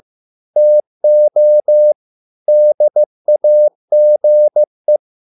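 Morse code, heard as a keyed tone, about 15 wpm, 600 Hz. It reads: TO DAGE
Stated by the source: Morse code at a slow 15 wpm from CW med Gnister.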